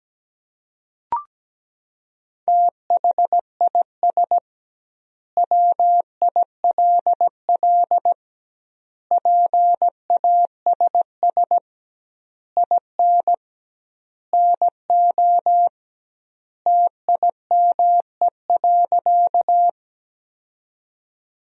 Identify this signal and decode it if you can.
Morse code, heard as a keyed tone, about 17 wpm, 700 Hz. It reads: THIS WILL PASS IN NO TIME.
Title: 2x Word-Spacing - Sentences from Top 400 Words - 17wpm